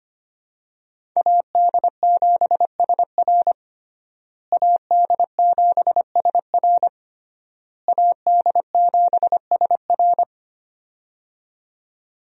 Morse code, heard as a keyed tone, about 25 wpm, 700 Hz. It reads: AD7SR AD7SR AD7SR
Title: USA Call Signs - 5 Characters - Repeated 3x - 25wpm